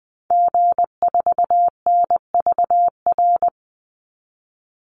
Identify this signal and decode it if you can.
Morse code, heard as a keyed tone, about 20 wpm, 700 Hz. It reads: G4NVR
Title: Call Signs Repeated 3x - 20wpm